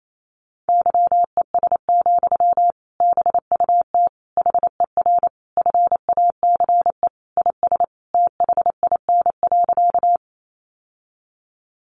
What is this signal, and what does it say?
Morse code, heard as a keyed tone, about 28 wpm, 700 Hz. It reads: YES, BUT HER FACE IS THIN.
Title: Sentences from Top 1200 Words 28wpm